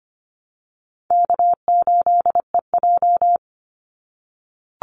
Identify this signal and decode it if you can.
Morse code, heard as a keyed tone, about 25 wpm, 700 Hz. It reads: K8EJ
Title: Easy Call Signs - 25wpm